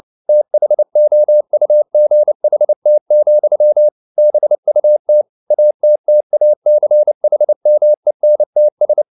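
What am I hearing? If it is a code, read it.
Morse code, heard as a keyed tone, about 29 wpm, 600 Hz. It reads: THOUGHT, BUT ATTACHMENTS
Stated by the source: Morse code at 29 wpm, sending Ken Follett's novel Never in English, from CW med Gnister.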